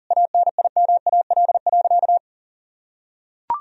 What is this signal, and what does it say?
Morse code, heard as a keyed tone, about 40 wpm, 700 Hz. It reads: ANIMAL.
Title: Sentences from Top 1000 Words - Review Part 1 - 40wpm